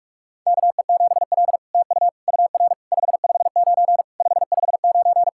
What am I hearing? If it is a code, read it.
Morse code, heard as a keyed tone, about 46 wpm, 700 Hz. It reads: KE7L TU UR 559 559
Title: SOTA QSO Practice 45wpm